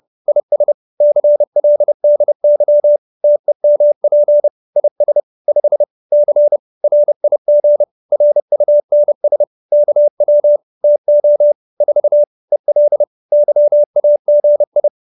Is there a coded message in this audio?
Yes: Morse code from 0.3 to 14.9 s, heard as a keyed tone, about 30 wpm, 590 Hz. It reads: IS CLDY TEMP IS 5 C RIG RUNS KW TO 4 EL YAGI